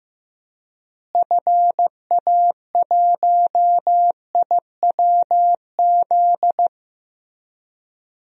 Morse code, heard as a keyed tone, about 15 wpm, 700 Hz. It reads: FA1IWZ
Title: DXCC-100 Call Signs - 6 Characters - Repeated 3x - 15wpm